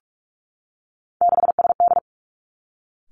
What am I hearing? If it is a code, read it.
Morse code, heard as a keyed tone, about 46 wpm, 700 Hz. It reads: 6SD
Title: Three Challenging Letter-Number Combinations 45wpm